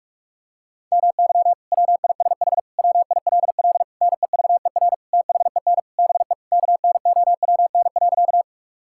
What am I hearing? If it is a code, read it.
Morse code, heard as a keyed tone, about 41 wpm, 700 Hz. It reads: MY WISH WILL NEVER THEN BE KNOWN.
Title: Sentences from Top 1600 Words 45wpm